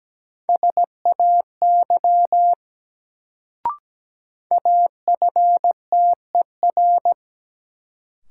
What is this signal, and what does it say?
Morse code, heard as a keyed tone, about 17 wpm, 700 Hz. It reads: SAY AFTER
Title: Sentences from Top 200 Words - New Word Drill - 17wpm